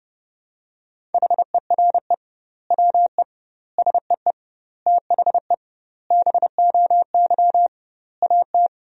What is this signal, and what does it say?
Morse code, heard as a keyed tone, about 32 wpm, 700 Hz. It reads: HERE WE SEE THE BOY AT
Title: 2x Word-Spacing - Sentences from Top 300 Words - 30wpm